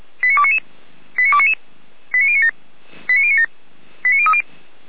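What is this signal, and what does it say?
Five-tone selective call (ZVEI), heard as quick runs of five tones, tones of about 70 ms. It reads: 89299 89299 89097 89097 89030
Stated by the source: ZVEI five-tone selective call 89294 received by SDR.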